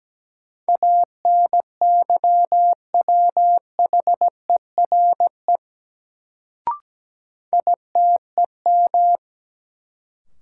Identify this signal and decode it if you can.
Morse code, heard as a keyed tone, about 17 wpm, 700 Hz. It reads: ANYWHERE ITEM
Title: Sentences from Top 1400 Words - New Word Drill Alt - 17wpm